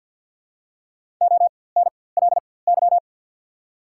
Morse code, heard as a keyed tone, about 54 wpm, spaced as for 24 wpm, 690 Hz. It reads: ONLY
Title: 4x Character Spacing - Top 500 Words - 50wpm